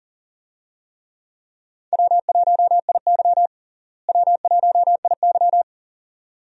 Morse code, heard as a keyed tone, about 40 wpm, 700 Hz. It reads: W1IY W1IY
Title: USA Call Signs - 4 Characters - Repeated 2x - 40wpm